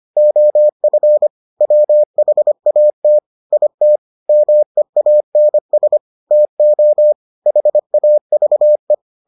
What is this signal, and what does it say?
Morse code, heard as a keyed tone, about 25 wpm, 600 Hz. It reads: OF WHAT IT MEANS TO HAVE